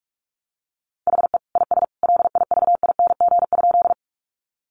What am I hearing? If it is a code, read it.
Morse code, heard as a keyed tone, about 44 wpm, 700 Hz. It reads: HE IS LIVING?
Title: Sentences from Top 1100 Words 45wpm